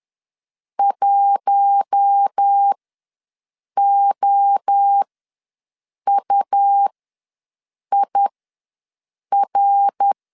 Morse code, heard as a keyed tone, about 10 wpm, spaced as for 6 wpm, 800 Hz. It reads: 1OUIR
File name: LEZIONI / 5 LBVC / 5 L ws 10wpm 3636